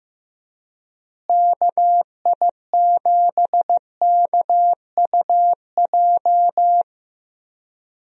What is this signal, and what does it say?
Morse code, heard as a keyed tone, about 15 wpm, 700 Hz. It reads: KI7KUJ
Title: USA Call Signs - 6 Characters - Repeated 3x - 15wpm